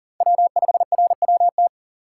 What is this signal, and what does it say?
Morse code, heard as a keyed tone, about 40 wpm, 700 Hz. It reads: W5RWT
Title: USA Call Signs - 5 Characters - 4x Character Spacing - 40wpm